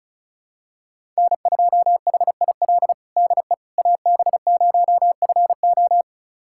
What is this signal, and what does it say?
Morse code, heard as a keyed tone, about 34 wpm, 700 Hz. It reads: N2HIL DE AB0FO
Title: USA Callsign DE Callsign - 5 Characters - 35wpm